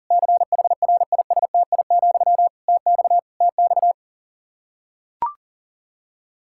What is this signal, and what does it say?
Morse code, heard as a keyed tone, about 37 wpm, 700 Hz. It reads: CHRISTI, TX TX
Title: K1USN SST QSO Practice 40wpm